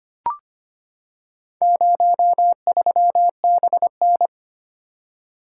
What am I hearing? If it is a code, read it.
Morse code, heard as a keyed tone, about 25 wpm, 700 Hz. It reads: 03BN
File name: practice / 25wpm / Four Letter-Number Combinations - 25wpm